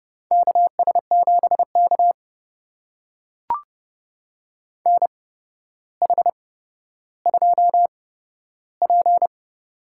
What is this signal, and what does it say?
Morse code, heard as a keyed tone, about 30 wpm, spaced as for 13 wpm, 700 Hz. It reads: KS7K NH2P
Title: USA Call Signs - 8x Character Spacing - 30wpm